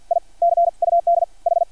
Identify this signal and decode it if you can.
Morse code, heard as a keyed tone, about 50 wpm, spaced as for 43 wpm, 680 Hz. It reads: I CUD H
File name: archived / CRSnet